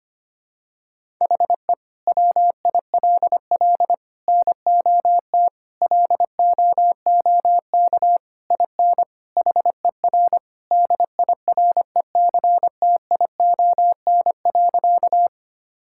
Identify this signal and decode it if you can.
Morse code, heard as a keyed tone, about 25 wpm, 700 Hz. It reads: HE WILL NOT LOOK IN HER DIRECTION.
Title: Sentences from Top 1000 Words - Review Part 5 - 25wpm